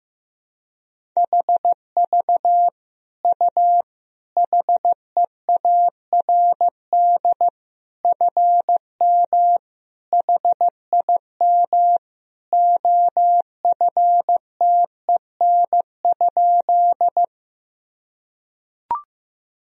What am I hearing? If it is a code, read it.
Morse code, heard as a keyed tone, about 15 wpm, 700 Hz. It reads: HV U HEARD FM HIM OFTEN?
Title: Sentences from Top 400 Words and Abbreviations 15wpm